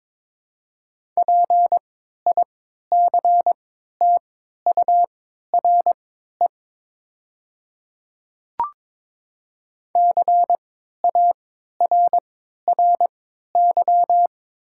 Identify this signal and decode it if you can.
Morse code, heard as a keyed tone, about 22 wpm, spaced as for 12 wpm, 700 Hz. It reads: PICTURE CARRY
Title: 3x Character Spacing - Top 300 Words - 22wpm